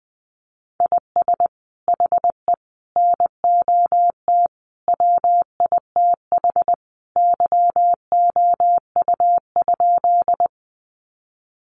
Morse code, heard as a keyed tone, about 20 wpm, 700 Hz. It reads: IS HE NOT WITH YOU?
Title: Sentences from Top 600 Words 20wpm